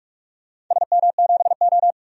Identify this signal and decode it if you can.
Morse code, heard as a keyed tone, about 48 wpm, 700 Hz. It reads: SM7O
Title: DXCC-100 Call Signs - 4 Characters - 45wpm